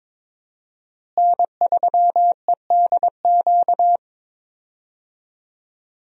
Morse code, heard as a keyed tone, about 22 wpm, 700 Hz. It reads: N3EDQ
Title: USA Call Signs - 5 Characters - Repeated 2x - 22wpm